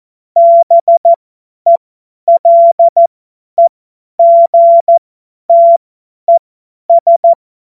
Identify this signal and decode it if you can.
Morse code, heard as a keyed tone, about 14 wpm, spaced as for 10 wpm, 680 Hz. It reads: BELEGTES
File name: Morsekurs / Witze / Witz 07.November 2023 50 F70